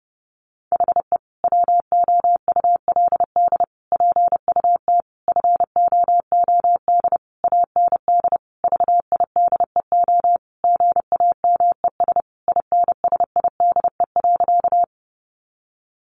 Morse code, heard as a keyed tone, about 30 wpm, 700 Hz. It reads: HE WOULD PUT FOOD AND VIDEO GAMES INSIDE.
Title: Easy Phrases 30wpm